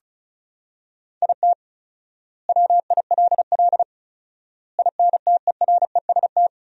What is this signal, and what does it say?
Morse code, heard as a keyed tone, about 34 wpm, 700 Hz. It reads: IT WILL INTEREST
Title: 4x Word-Spacing - Sentences from Top 400 Words - 35wpm